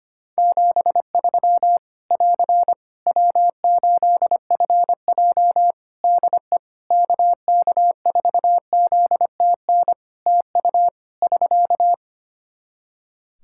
Morse code, heard as a keyed tone, about 25 wpm, 700 Hz. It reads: 73 <AR> W8FJ DE KK4ZTN TU <SK>
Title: Practice QSO - Exchange 4 - Ending - 25wpm